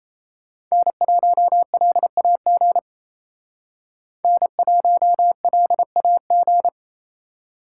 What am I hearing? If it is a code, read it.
Morse code, heard as a keyed tone, about 32 wpm, 700 Hz. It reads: N1LAG N1LAG
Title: Speed Racing - USA 1x3 Call Signs 28wpm